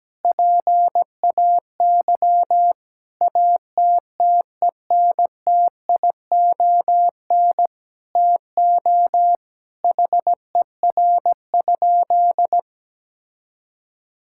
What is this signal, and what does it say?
Morse code, heard as a keyed tone, about 17 wpm, 700 Hz. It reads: PAY ATTENTION TO HER?
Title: Sentences from Top 1000 Words - Review Part 6 - 17wpm